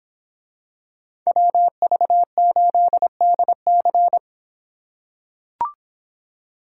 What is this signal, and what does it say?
Morse code, heard as a keyed tone, about 26 wpm, 700 Hz. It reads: WV8DC